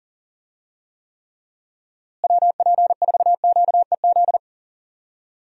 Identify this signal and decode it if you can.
Morse code, heard as a keyed tone, about 40 wpm, 700 Hz. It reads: WP4QEZ
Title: USA Call Signs - 6 Characters - Repeated 2x - 40wpm